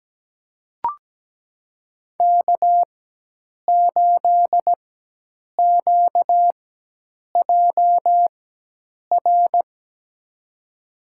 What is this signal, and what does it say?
Morse code, heard as a keyed tone, about 17 wpm, spaced as for 8 wpm, 700 Hz. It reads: K8QJR
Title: USA Call Signs - 5 Characters - 4x Character Spacing - 17wpm